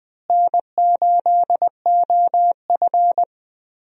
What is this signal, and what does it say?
Morse code, heard as a keyed tone, about 20 wpm, 700 Hz. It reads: N8OF
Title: USA Call Signs - 4 Characters - Repeated 3x - 20wpm